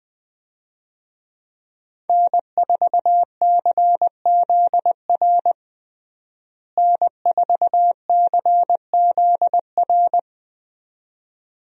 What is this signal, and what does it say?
Morse code, heard as a keyed tone, about 20 wpm, 700 Hz. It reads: N4CZR N4CZR
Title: USA Call Signs - 5 Characters - Repeated 2x - 20wpm